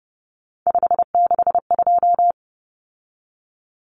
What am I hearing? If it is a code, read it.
Morse code, heard as a keyed tone, about 30 wpm, 700 Hz. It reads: 562
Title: Common Cut Numbers - 3-Digit 30wpm